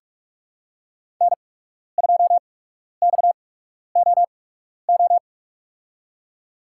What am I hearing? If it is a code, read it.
Morse code, heard as a keyed tone, about 45 wpm, spaced as for 12 wpm, 700 Hz. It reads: N2XOO
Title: USA Call Signs - 5 Characters - 8x Character Spacing - 45wpm